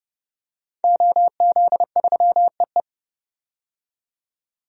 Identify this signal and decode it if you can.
Morse code, heard as a keyed tone, about 32 wpm, 700 Hz. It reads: OZ3EE
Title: DXCC-100 Call Signs - 5 Characters - 30wpm